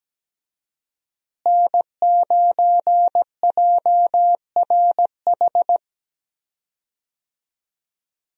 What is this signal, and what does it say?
Morse code, heard as a keyed tone, about 17 wpm, 700 Hz. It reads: N9JRH
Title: USA Call Signs - 5 Characters - Repeated 2x - 17wpm